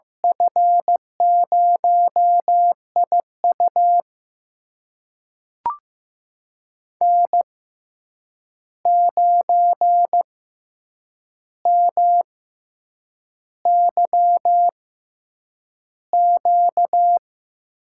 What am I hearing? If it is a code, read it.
Morse code, heard as a keyed tone, about 15 wpm, spaced as for 7 wpm, 700 Hz. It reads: F0IU N9MYQ
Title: USA Call Signs - 5 Characters - 6x Character Spacing - 15wpm